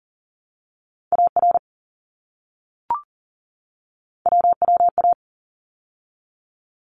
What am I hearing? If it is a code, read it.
Morse code, heard as a keyed tone, about 40 wpm, 700 Hz. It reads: AR WWA
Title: ARRL Sections 40wpm